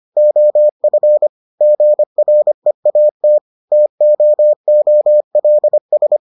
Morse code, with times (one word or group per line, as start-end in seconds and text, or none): 0.2-1.3 OF
1.6-3.4 GREAT
3.7-6.2 TOOLS